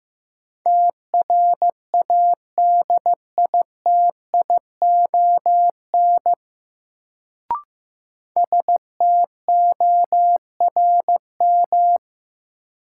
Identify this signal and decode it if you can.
Morse code, heard as a keyed tone, about 15 wpm, 700 Hz. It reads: TRADITION STORM
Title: Sentences from Top 1500 Words - New Word Drill - 15wpm